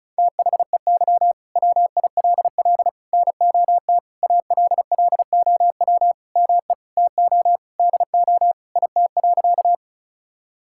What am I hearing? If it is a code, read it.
Morse code, heard as a keyed tone, about 34 wpm, 700 Hz. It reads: THEY WILL NOT ALLOW ME TO DO IT.
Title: Sentences from Top 1000 Words - Review Part 5 - 35wpm